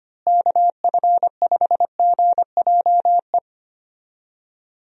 Morse code, heard as a keyed tone, about 25 wpm, 700 Hz. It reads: KF5GJE
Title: USA Call Signs - 6 Characters - Repeated 3x - 25wpm